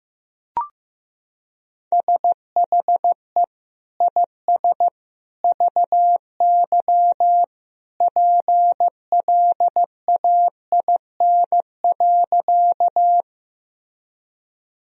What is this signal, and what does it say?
Morse code, heard as a keyed tone, about 15 wpm, 700 Hz. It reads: SHE IS VY PLAIN.
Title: Sentences from Top 400 Words and Abbreviations 15wpm